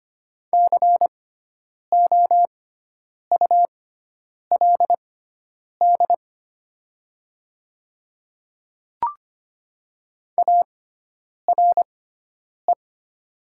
Morse code, heard as a keyed tone, about 25 wpm, spaced as for 9 wpm, 700 Hz. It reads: COULD ARE
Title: 6x Character Spacing - Top 100 Words - 25wpm